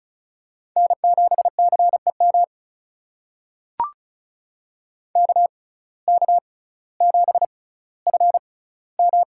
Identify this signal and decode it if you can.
Morse code, heard as a keyed tone, about 36 wpm, spaced as for 18 wpm, 700 Hz. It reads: N7CEM KK7FM
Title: USA Call Signs - 5 Characters - 6x Character Spacing - 35wpm